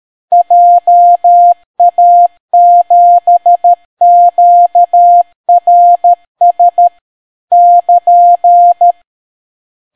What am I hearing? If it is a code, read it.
Morse code, heard as a keyed tone, about 13 wpm, 700 Hz. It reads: JA7QRS <KN>